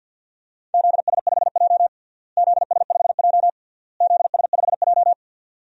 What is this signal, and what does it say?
Morse code, heard as a keyed tone, about 48 wpm, 700 Hz. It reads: ZS5J ZS5J ZS5J
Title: DXCC-100 Call Signs - 4 Characters - Repeated 3x - 50wpm